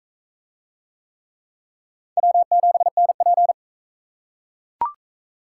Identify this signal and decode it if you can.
Morse code, heard as a keyed tone, about 37 wpm, 700 Hz. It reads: W7NP